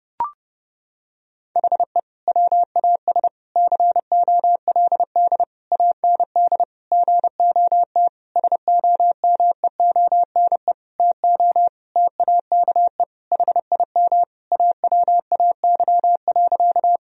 Morse code, heard as a keyed tone, about 30 wpm, 700 Hz. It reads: HE WAS COLD AND GOT SOMEONE TO TAKE HIM AWAY.
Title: Sentences from Top 1000 Words - Review Part 4 - 30wpm